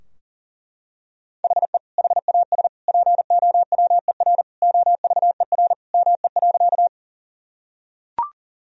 Morse code, heard as a keyed tone, about 40 wpm, 700 Hz. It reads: HE HAS POWER OVER ME.
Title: Sentences from Top 1000 Words - Review Part 2 - 40wpm